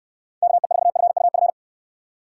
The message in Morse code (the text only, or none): KE4FUF